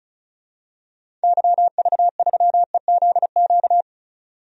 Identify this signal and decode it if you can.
Morse code, heard as a keyed tone, about 36 wpm, 700 Hz. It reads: YV3EZQ
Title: DXCC-100 Call Signs - 6 Characters - 35wpm